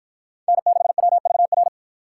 Morse code, heard as a keyed tone, about 47 wpm, 700 Hz. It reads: N6WVR